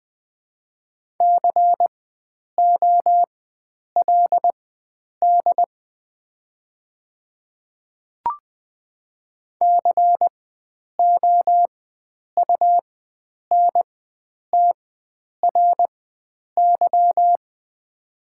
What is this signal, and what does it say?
Morse code, heard as a keyed tone, about 20 wpm, spaced as for 9 wpm, 700 Hz. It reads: COLD COUNTRY
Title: 4x Character Spacing - Top 500 Words - 20wpm